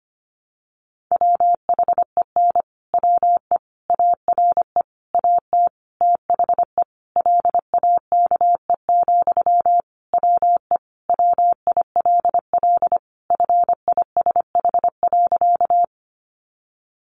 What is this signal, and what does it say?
Morse code, heard as a keyed tone, about 25 wpm, 700 Hz. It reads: WHEN WE ARE AT THE LAKE, WE WILL FISH.
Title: Sentences from Top 1100 Words 25wpm